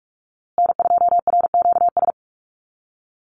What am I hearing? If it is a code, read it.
Morse code, heard as a keyed tone, about 44 wpm, 690 Hz. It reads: N2RQS
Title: USA Call Signs Repeated 2x 45wpm